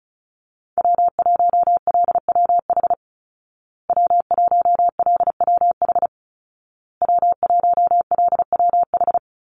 Morse code, heard as a keyed tone, about 34 wpm, 700 Hz. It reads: W1LWH W1LWH W1LWH
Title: Call Signs Repeated 2x - 35wpm